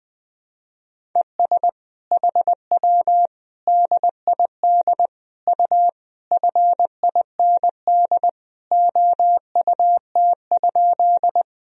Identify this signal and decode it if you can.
Morse code, heard as a keyed tone, about 20 wpm, 700 Hz. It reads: ES HW DID U FIND OUT?
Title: Sentences from Top 200 Words and Abbreviations 20wpm